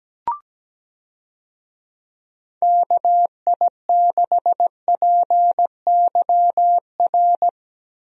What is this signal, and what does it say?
Morse code, heard as a keyed tone, about 17 wpm, 700 Hz. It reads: KI6PYR
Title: USA Call Signs - 6 Characters - Repeated 2x - 17wpm